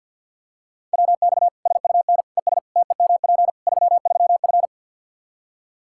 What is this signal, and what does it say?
Morse code, heard as a keyed tone, about 50 wpm, 700 Hz. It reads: WX SUN ES TEMP 33F
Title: K1USN SST QSO Practice 50wpm